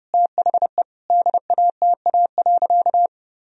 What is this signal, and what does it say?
Morse code, heard as a keyed tone, about 30 wpm, 700 Hz. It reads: THE DATA.
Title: Sentences from Top 1000 Words - Review Part 3 - 30wpm